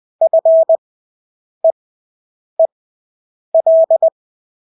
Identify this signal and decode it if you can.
Morse code, heard as a keyed tone, about 20 wpm, spaced as for 8 wpm, 650 Hz. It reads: FEEL
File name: Engelske ord 20/8 wpm